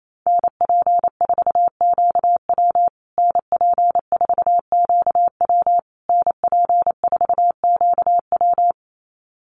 Morse code, heard as a keyed tone, about 28 wpm, 700 Hz. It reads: NP4QW NP4QW NP4QW
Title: USA Call Signs Repeated 3x 28wpm